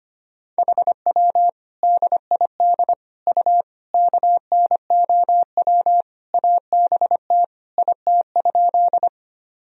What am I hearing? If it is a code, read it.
Morse code, heard as a keyed tone, about 25 wpm, 700 Hz. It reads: HW DID U KNOW ABT IT?
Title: Sentences from Top 100 Words and Abbreviations 25wpm